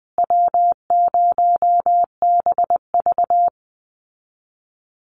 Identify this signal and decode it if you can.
Morse code, heard as a keyed tone, about 19 wpm, 700 Hz. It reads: W0BV
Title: Easy Call Signs Repeated 2x - 20wpm